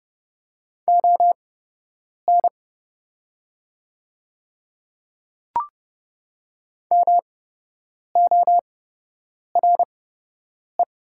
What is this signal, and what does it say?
Morse code, heard as a keyed tone, about 30 wpm, spaced as for 8 wpm, 700 Hz. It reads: ON MORE
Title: 8x Character Spacing - Top 100 Words - 30wpm